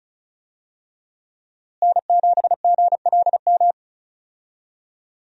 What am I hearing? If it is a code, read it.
Morse code, heard as a keyed tone, about 36 wpm, 700 Hz. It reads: N7GLM
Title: USA Call Signs - 5 Characters - Repeated 2x - 35wpm